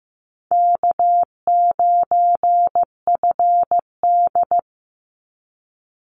Morse code, heard as a keyed tone, about 15 wpm, 700 Hz. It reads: K9FD
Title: Easy Call Signs - 15wpm